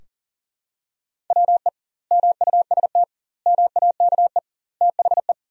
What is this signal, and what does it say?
Morse code, heard as a keyed tone, about 40 wpm, 700 Hz. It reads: WE MUST MAKE THE
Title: 2x Word-Spacing - Sentences from Top 400 Words - 40wpm